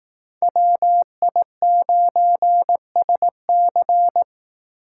Morse code, heard as a keyed tone, about 18 wpm, 700 Hz. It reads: WI9SC
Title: Speed Racing - USA Call Signs 15wpm